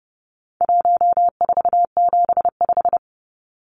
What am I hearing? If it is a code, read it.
Morse code, heard as a keyed tone, about 30 wpm, 700 Hz. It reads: 1475
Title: Numbers - 4-Digit 30wpm